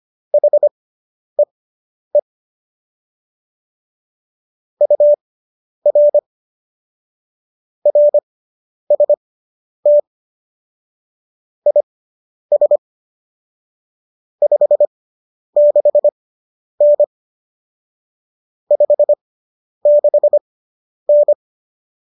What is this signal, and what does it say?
Morse code, heard as a keyed tone, about 25 wpm, spaced as for 10 wpm, 590 Hz. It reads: HEE UR RST IS 56N 56N